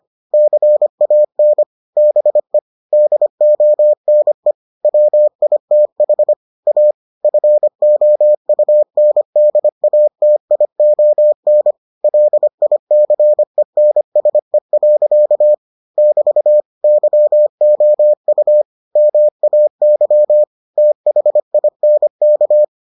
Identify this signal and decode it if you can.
Morse code, heard as a keyed tone, about 25 wpm, 600 Hz. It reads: CAN BE DONE WITH A FOUNDATION LICENSE. <BT> YOU MAY THINK